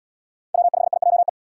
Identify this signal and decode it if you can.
Morse code, heard as a keyed tone, about 76 wpm, 700 Hz. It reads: W5IJE